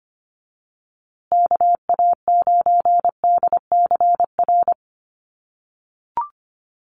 Morse code, heard as a keyed tone, about 25 wpm, 700 Hz. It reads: KA9DCR
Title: USA Call Signs Repeated 2x 25wpm